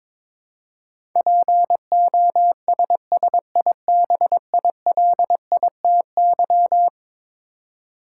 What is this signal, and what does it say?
Morse code, heard as a keyed tone, about 22 wpm, 700 Hz. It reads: POSSIBILITY
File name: practice / 22wpm / Eleven Letter Words - 22wpm